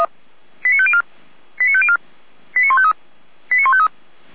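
Five-tone selective call (ZVEI), heard as quick runs of five tones, tones of about 70 ms. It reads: 89504 89504 89153 89153
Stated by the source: ZVEI five-tone alert received by SDR, from Feuerwehr Orlamünde's audio archive.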